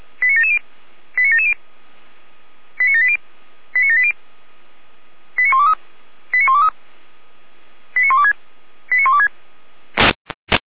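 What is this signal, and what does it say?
Five-tone selective call (ZVEI), heard as quick runs of five tones, tones of about 70 ms. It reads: 89770 89770 89780 89780 89123 89123 89126 89126
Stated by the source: ZVEI five-tone selective call received with an SDR.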